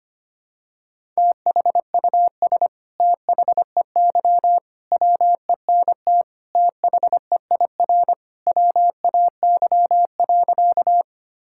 Morse code, heard as a keyed tone, about 25 wpm, 700 Hz. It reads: THUS THEY WENT THEIR WAY.
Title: Sentences from Top 1000 Words - Review Part 5 - 25wpm